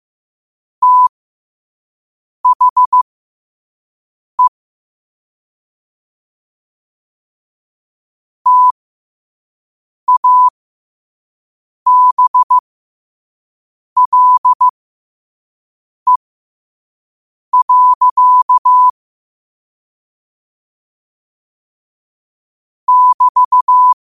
Morse code, heard as a keyed tone, about 15 wpm, spaced as for 5 wpm, 1000 Hz. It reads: THE TABLE. <BT>